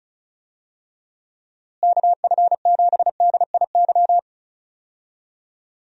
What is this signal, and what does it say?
Morse code, heard as a keyed tone, about 36 wpm, 700 Hz. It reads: KF7DIY
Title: USA Call Signs - 6 Characters - Repeated 2x - 35wpm